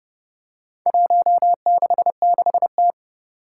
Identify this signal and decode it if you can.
Morse code, heard as a keyed tone, about 30 wpm, 700 Hz. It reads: 166T
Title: Four Challenging Letter-Number Combinations 30wpm